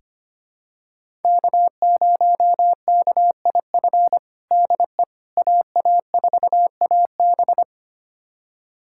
Morse code, heard as a keyed tone, about 25 wpm, 700 Hz. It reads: K0KIF DE AA4AB
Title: USA Callsign DE Callsign - 5 Characters - 25wpm